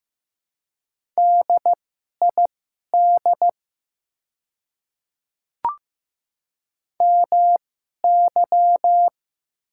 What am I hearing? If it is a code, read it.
Morse code, heard as a keyed tone, about 15 wpm, spaced as for 11 wpm, 700 Hz. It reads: DID MY
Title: 2x Character Spacing - Top 100 Words - 15wpm